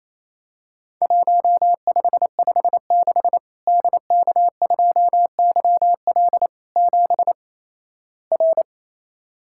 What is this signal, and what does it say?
Morse code, heard as a keyed tone, about 28 wpm, 700 Hz. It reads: 1556 DK2YL 7 R
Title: Worked All Europe DX Contest - QTC Exchange - 28wpm